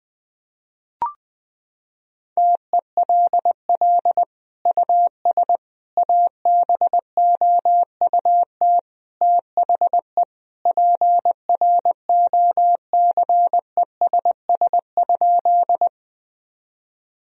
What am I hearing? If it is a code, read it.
Morse code, heard as a keyed tone, about 20 wpm, 700 Hz. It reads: TELL US ABOUT THE PROCESS?
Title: Sentences from Top 1000 Words - Review Part 3 - 20wpm